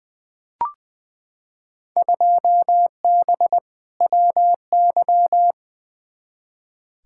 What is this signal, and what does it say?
Morse code, heard as a keyed tone, about 20 wpm, 700 Hz. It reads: 2B WY
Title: Field Day Exchanges Encore - 20wpm